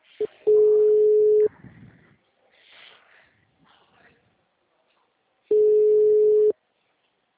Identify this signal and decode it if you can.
Ringback tone (Europe, 425 Hz), with speech.
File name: out- -2022- - - .59266